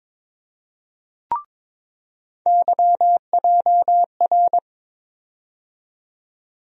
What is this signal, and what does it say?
Morse code, heard as a keyed tone, about 22 wpm, 700 Hz. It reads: YJR
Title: USA Call Sign Postfixes - 3 Letters 22wpm